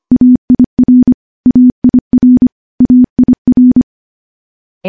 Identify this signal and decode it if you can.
Morse code, heard as a keyed tone, about 25 wpm, 250 Hz. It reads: AIR AIR AIR
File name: morse / morse-training-25WPM